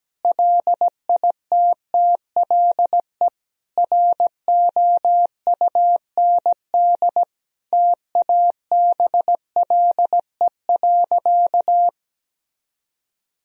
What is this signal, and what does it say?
Morse code, heard as a keyed tone, about 17 wpm, 700 Hz. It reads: LITTLE ROUND TABLE.
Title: Sentences from Top 1000 Words - Review Part 2 - 17wpm